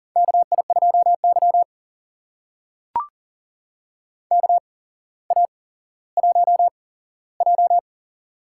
Morse code, heard as a keyed tone, about 40 wpm, spaced as for 17 wpm, 700 Hz. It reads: KI2Y KA1J
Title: USA Call Signs - 4 Characters - 8x Character Spacing - 40wpm